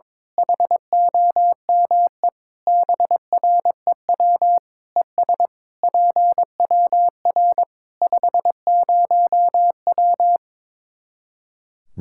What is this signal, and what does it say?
Morse code, heard as a keyed tone, about 22 wpm, 700 Hz. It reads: HOME BREW ES PWR 50W
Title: Practice QSO - Exchange 2 - Info - 22wpm